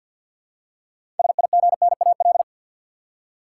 Morse code, heard as a keyed tone, about 52 wpm, 700 Hz. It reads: SIGNAL
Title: Six Letter Words - 50wpm